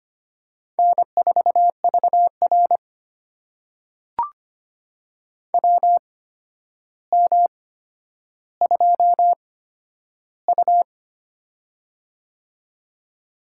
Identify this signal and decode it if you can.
Morse code, heard as a keyed tone, about 25 wpm, spaced as for 11 wpm, 700 Hz. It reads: N4VR WM2U
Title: USA Call Signs - 4 Characters - 8x Character Spacing - 25wpm